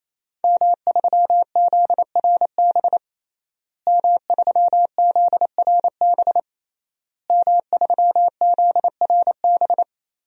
Morse code, heard as a keyed tone, about 28 wpm, 700 Hz. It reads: M3ZRB M3ZRB M3ZRB
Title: DXCC-100 Call Signs - 5 Characters - Repeated 3x - 28wpm